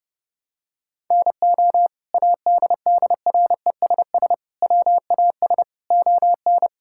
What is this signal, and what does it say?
Morse code, heard as a keyed tone, about 30 wpm, 700 Hz. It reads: NO ADDRESS WAS ON
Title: Sentences from Top 1000 Words - Review Part 6 - 30wpm